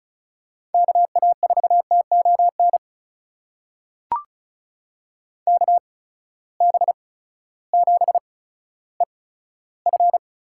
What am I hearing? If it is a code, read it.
Morse code, heard as a keyed tone, about 36 wpm, spaced as for 16 wpm, 700 Hz. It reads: KA4TON KB7EF
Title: USA Call Signs - 6 Characters - 8x Character Spacing - 35wpm